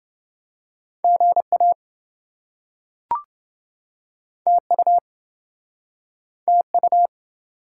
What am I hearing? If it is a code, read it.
Morse code, heard as a keyed tone, about 27 wpm, 700 Hz. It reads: GA TU TU